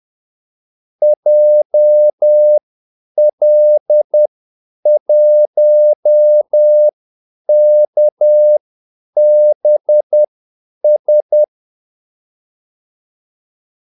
Morse code, heard as a keyed tone, about 10 wpm, spaced as for 8 wpm, 600 Hz. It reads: JL1KBS
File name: Callsigns-10-8wpm